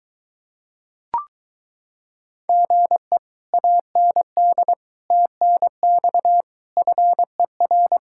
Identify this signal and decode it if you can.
Morse code, heard as a keyed tone, about 22 wpm, 700 Hz. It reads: GE AND TNX FER